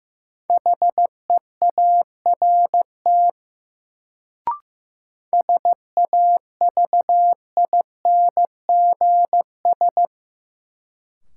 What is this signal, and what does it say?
Morse code, heard as a keyed tone, about 15 wpm, 700 Hz. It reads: HEART SAVINGS
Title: Sentences from Top 1200 Words - New Word Drill Alt - 15wpm